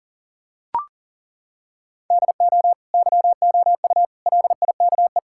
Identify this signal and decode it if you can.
Morse code, heard as a keyed tone, about 40 wpm, 700 Hz. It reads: DO YOU LIKE
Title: Sentences from Top 1000 Words - Review Part 5 - 40wpm